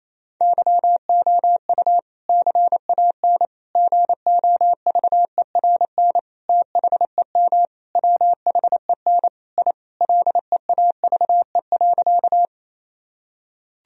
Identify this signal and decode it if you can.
Morse code, heard as a keyed tone, about 28 wpm, 700 Hz. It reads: YOU CAN GOVERN THEM WHEN I LEAVE.
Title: Sentences from Top 1000 Words - Review Part 2 - 28wpm